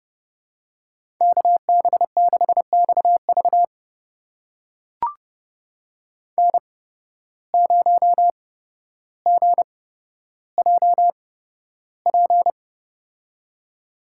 Morse code, heard as a keyed tone, about 32 wpm, spaced as for 13 wpm, 700 Hz. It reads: KB6XV N0GJP
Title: USA Call Signs - 5 Characters - 8x Character Spacing - 30wpm